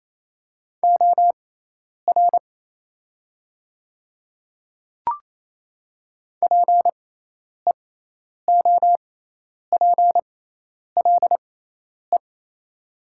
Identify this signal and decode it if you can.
Morse code, heard as a keyed tone, about 27 wpm, spaced as for 10 wpm, 700 Hz. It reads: OR PEOPLE
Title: 6x Character Spacing - Top 100 Words - 28wpm